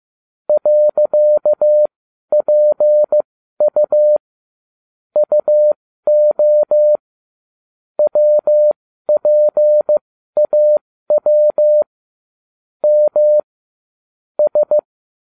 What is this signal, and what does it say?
Morse code, heard as a keyed tone, about 15 wpm, spaced as for 12 wpm, 600 Hz. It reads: .PU UO WPAW M S
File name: cw / ubacw12 2